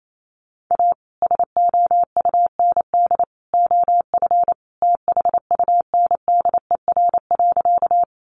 Morse code, heard as a keyed tone, about 28 wpm, 700 Hz. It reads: A SOUND OF THUNDER.
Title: A Sound Of Thunder 28wpm